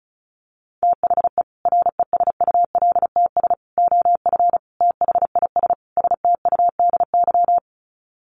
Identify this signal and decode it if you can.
Morse code, heard as a keyed tone, about 34 wpm, 700 Hz. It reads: THE RESULTS OF THIS STUDY
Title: Sets of 5 Words 35wpm